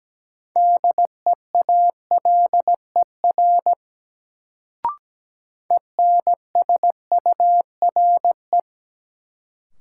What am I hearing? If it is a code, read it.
Morse code, heard as a keyed tone, about 17 wpm, 700 Hz. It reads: DEALER ENSURE
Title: Sentences from Top 1100 Words - New Word Drill - 17wpm